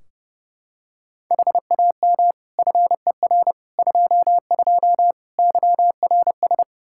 Morse code, heard as a keyed tone, about 32 wpm, 700 Hz. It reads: HAM FER 22 YRS
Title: Practice QSO - Exchange 3 - More Info - 30wpm